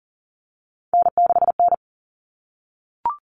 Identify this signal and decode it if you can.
Morse code, heard as a keyed tone, about 40 wpm, 700 Hz. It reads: N6N
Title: All Cut Numbers - 3-Digit 40wpm